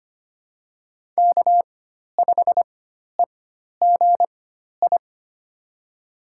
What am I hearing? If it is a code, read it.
Morse code, heard as a keyed tone, about 25 wpm, spaced as for 12 wpm, 700 Hz. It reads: K5EGI
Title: USA Call Signs - 4x Character Spacing - 25wpm